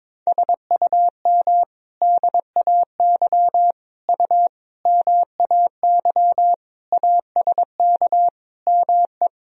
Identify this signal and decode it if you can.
Morse code, heard as a keyed tone, about 22 wpm, 700 Hz. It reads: SUM DAY U MAY ASK ME
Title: Sentences from Top 200 Words and Abbreviations 22wpm